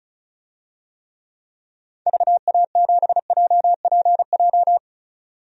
Morse code, heard as a keyed tone, about 36 wpm, 690 Hz. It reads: VA7JPJ
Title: DXCC-100 Call Signs - 6 Characters - 35wpm